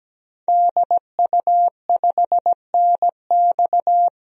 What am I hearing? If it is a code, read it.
Morse code, heard as a keyed tone, about 17 wpm, 700 Hz. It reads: DU5NX
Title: DXCC-100 Call Signs - 5 Characters - 17wpm